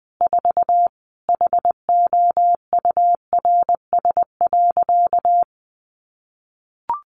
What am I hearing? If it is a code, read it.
Morse code, heard as a keyed tone, about 20 wpm, 700 Hz. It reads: 4 HOURS.
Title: Additional Easy Phrases 20wpm